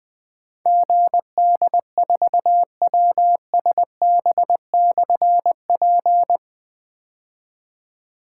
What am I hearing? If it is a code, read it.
Morse code, heard as a keyed tone, about 20 wpm, 700 Hz. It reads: GD4WSB/P
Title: Difficult Call Signs 20wpm